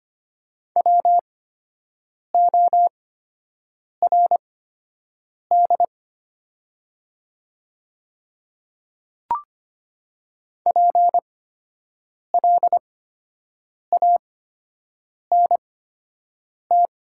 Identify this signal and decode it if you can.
Morse code, heard as a keyed tone, about 25 wpm, spaced as for 7 wpm, 700 Hz. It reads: WORD PLANT
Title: 8x Character Spacing - Top 300 Words - 25wpm